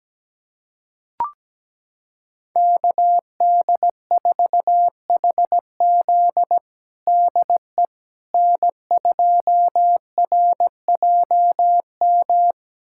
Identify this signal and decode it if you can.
Morse code, heard as a keyed tone, about 17 wpm, 700 Hz. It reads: KD4HZ DE N2RJM
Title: USA Callsign DE Callsign - 5 Characters - 17wpm